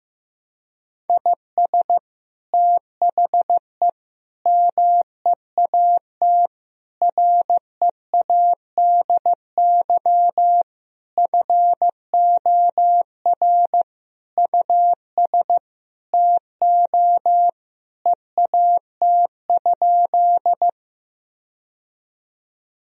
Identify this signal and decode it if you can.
Morse code, heard as a keyed tone, about 15 wpm, 700 Hz. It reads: IS THE MEAT READY FOR US TO EAT?
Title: Sentences from Top 1000 Words - Review Part 3 - 15wpm